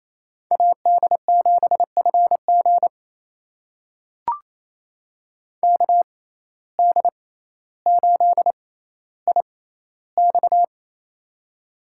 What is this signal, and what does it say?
Morse code, heard as a keyed tone, about 28 wpm, spaced as for 14 wpm, 700 Hz. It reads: AD7FG KD8IX
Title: USA Call Signs - 5 Characters - 6x Character Spacing - 28wpm